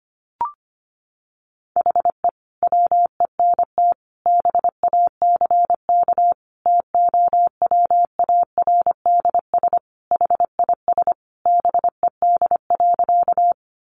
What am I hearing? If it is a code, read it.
Morse code, heard as a keyed tone, about 25 wpm, 700 Hz. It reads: HE WENT BACK TOWARDS HIS BED.
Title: Sentences from Top 1100 Words 25wpm